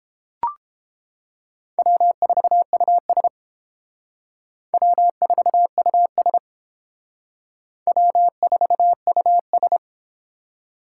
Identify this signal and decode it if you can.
Morse code, heard as a keyed tone, about 28 wpm, 700 Hz. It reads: W4US W4US W4US